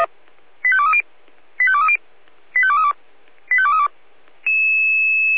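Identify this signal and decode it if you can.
Five-tone selective call (ZVEI), heard as quick runs of five tones, tones of about 70 ms. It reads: 85230 85230 85232 85232
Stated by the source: ZVEI five-tone selective call received on SDR.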